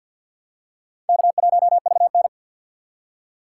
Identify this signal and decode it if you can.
Morse code, heard as a keyed tone, about 50 wpm, 700 Hz. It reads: K1VN
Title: USA Call Signs - 4 Characters - 2x Character Spacing - 50wpm